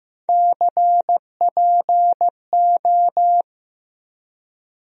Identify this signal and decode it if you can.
Morse code, heard as a keyed tone, about 15 wpm, 700 Hz. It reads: CPO
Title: USA Call Sign Postfixes - 3 Letters 15wpm